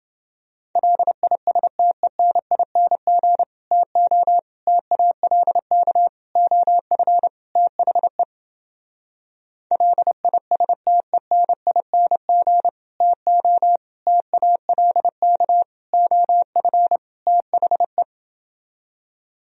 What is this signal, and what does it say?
Morse code, heard as a keyed tone, about 30 wpm, 700 Hz. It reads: LISTENING TO TALK OF THE LISTENING TO TALK OF THE